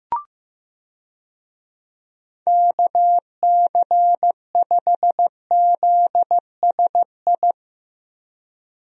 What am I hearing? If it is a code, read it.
Morse code, heard as a keyed tone, about 15 wpm, 700 Hz. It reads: KC5ZSI
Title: USA Call Signs - 6 Characters - Repeated 2x - 15wpm